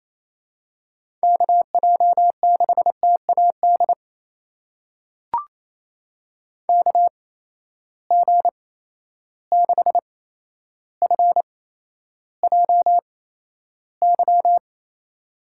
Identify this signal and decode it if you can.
Morse code, heard as a keyed tone, about 28 wpm, spaced as for 12 wpm, 700 Hz. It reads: KJ6TAD KG6FJY
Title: USA Call Signs - 6 Characters - 8x Character Spacing - 28wpm